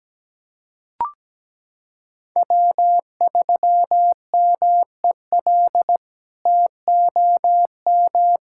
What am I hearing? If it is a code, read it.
Morse code, heard as a keyed tone, about 17 wpm, 700 Hz. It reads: W3MEL TOM
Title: K1USN SST QSO Practice - 17wpm